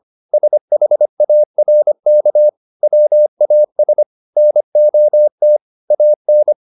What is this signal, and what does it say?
Morse code, heard as a keyed tone, about 25 wpm, 600 Hz. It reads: SHARK WAS NOT AN